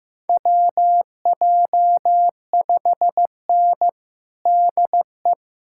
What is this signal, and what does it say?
Morse code, heard as a keyed tone, about 15 wpm, 700 Hz. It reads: WJ5N DE
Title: USA Callsign DE Callsign - 4 Characters - 15wpm